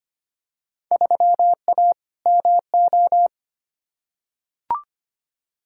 Morse code, heard as a keyed tone, about 25 wpm, 700 Hz. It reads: 3A MO
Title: Field Day Exchanges Encore - 25wpm